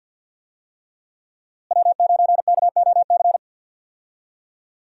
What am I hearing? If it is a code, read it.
Morse code, heard as a keyed tone, about 46 wpm, 700 Hz. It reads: W9KOC
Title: USA Call Signs - 5 Characters - Repeated 3x - 50wpm